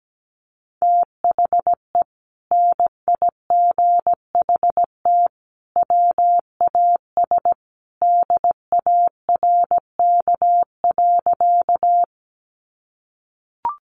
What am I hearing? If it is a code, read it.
Morse code, heard as a keyed tone, about 17 wpm, 700 Hz. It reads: THE NIGHT WAS DARK.
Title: Sentences from Top 500 Words 17wpm